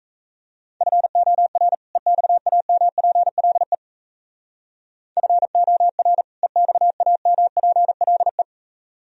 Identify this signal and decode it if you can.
Morse code, heard as a keyed tone, about 33 wpm, 700 Hz. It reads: FOR EXAMPLE FOR EXAMPLE
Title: Speed-Racing - Sets of 2 Words 28wpm